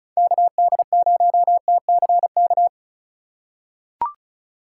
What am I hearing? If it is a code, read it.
Morse code, heard as a keyed tone, about 36 wpm, 700 Hz. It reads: KD0TCK
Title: USA Call Signs - 6 Characters - 3x Character Spacing - 35wpm